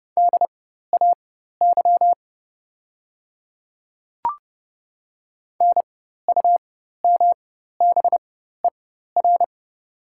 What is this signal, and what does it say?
Morse code, heard as a keyed tone, about 30 wpm, spaced as for 14 wpm, 700 Hz. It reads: DAY NUMBER
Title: 4x Character Spacing - Top 100 Words - 30wpm